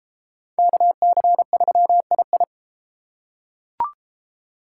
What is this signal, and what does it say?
Morse code, heard as a keyed tone, about 33 wpm, 700 Hz. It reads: KC3II